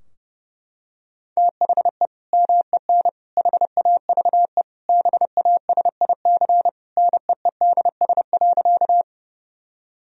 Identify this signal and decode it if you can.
Morse code, heard as a keyed tone, about 30 wpm, 700 Hz. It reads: THE MEN HAVE BASIC NEEDS.
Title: Sentences from Top 1000 Words - Review Part 3 - 30wpm